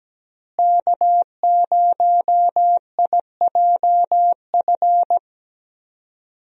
Morse code, heard as a keyed tone, about 17 wpm, 700 Hz. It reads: K0IJF
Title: USA Call Signs - 5 Characters - Repeated 3x - 17wpm